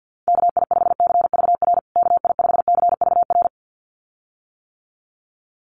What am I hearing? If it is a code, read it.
Morse code, heard as a keyed tone, about 50 wpm, 700 Hz. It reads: KI5CVR KI5CVR
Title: USA Call Signs Repeated 2x 50wpm